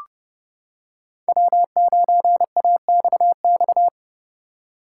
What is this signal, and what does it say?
Morse code, heard as a keyed tone, about 30 wpm, 700 Hz. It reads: W9AXX